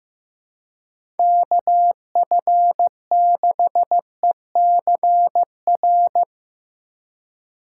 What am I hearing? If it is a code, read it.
Morse code, heard as a keyed tone, about 15 wpm, 700 Hz. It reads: KF6ECR
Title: USA Call Signs - 6 Characters - 2x Character Spacing - 15wpm